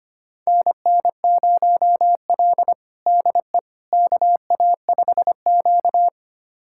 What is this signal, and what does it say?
Morse code, heard as a keyed tone, about 25 wpm, 700 Hz. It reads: NN0L DE KA5Q
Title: USA Callsign DE Callsign - 4 Characters - 25wpm